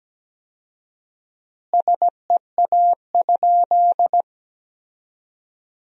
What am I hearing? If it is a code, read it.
Morse code, heard as a keyed tone, about 17 wpm, 700 Hz. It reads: SEA?
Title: 4x Word-Spacing - Sentences from Top 300 Words - 17wpm